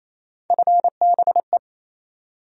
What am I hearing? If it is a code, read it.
Morse code, heard as a keyed tone, about 28 wpm, 700 Hz. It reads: FBE
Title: USA Call Sign Postfixes - 3 Letters 28wpm